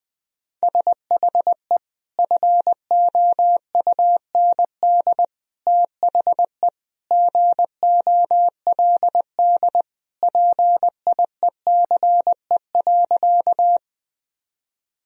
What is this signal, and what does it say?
Morse code, heard as a keyed tone, about 20 wpm, 700 Hz. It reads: SHE FOUND THE GOLD PIECE.
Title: Sentences from Top 1000 Words - Review Part 3 - 20wpm